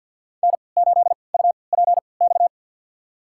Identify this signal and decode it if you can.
Morse code, heard as a keyed tone, about 52 wpm, spaced as for 29 wpm, 700 Hz. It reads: N8UPX
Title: USA Call Signs - 5 Characters - 3x Character Spacing - 50wpm